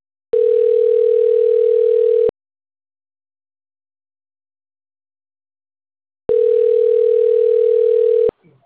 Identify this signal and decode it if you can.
Ringback tone (North America), 440 + 480 Hz.
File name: autotestagent / 11-21-31.136..callcentertransfer. .6572df23-3f41-4b03-b195